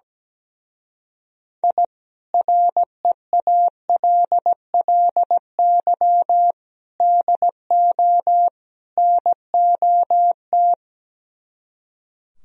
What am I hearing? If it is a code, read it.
Morse code, heard as a keyed tone, about 17 wpm, 700 Hz. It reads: I REALLY DO NOT
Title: Speed-Racing - Sets of 4 Words 17wpm